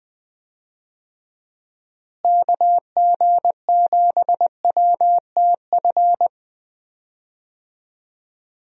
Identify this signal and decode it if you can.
Morse code, heard as a keyed tone, about 20 wpm, 700 Hz. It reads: KG7WTF
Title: USA Call Signs - 6 Characters - Repeated 2x - 20wpm